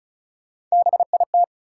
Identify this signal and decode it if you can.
Morse code, heard as a keyed tone, about 36 wpm, 700 Hz. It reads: BIT
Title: Sentences from Top 900 Words - New Word Drill - 35wpm